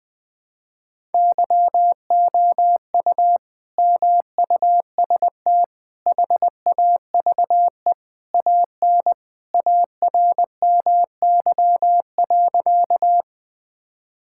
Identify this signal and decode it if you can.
Morse code, heard as a keyed tone, about 20 wpm, 700 Hz. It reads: YOU MUST HAVE AN ARMY.
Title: Sentences from Top 1000 Words - Review Part 5 - 20wpm